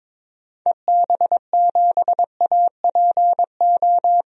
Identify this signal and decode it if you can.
Morse code, heard as a keyed tone, about 21 wpm, 700 Hz. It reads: EB7APO
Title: DXCC-100 Call Signs - 6 Characters - 22wpm